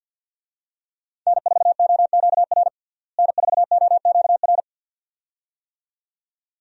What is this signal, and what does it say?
Morse code, heard as a keyed tone, about 47 wpm, 700 Hz. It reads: N4OQR N4OQR
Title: USA Call Signs - 5 Characters - Repeated 2x - 50wpm